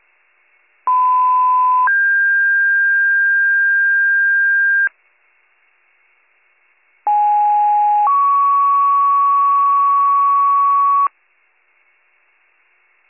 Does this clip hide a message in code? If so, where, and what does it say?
Yes: two-tone page from 0.9 to 11.1 s, heard as a one-second tone, then a three-second tone, about 1010 then 1600 Hz, about 830 then 1120 Hz.